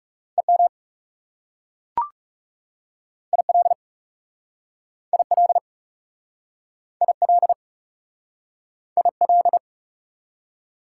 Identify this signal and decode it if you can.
Morse code, heard as a keyed tone, about 40 wpm, 690 Hz. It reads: EM IL IL IL IL